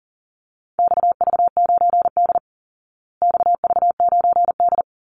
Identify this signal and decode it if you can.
Morse code, heard as a keyed tone, about 40 wpm, 700 Hz. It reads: XV9D XV9D
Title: Call Signs Repeated 3x - 40wpm